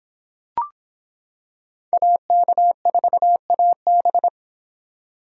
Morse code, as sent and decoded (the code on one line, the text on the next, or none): .- -.- ....- .- -...
AK4AB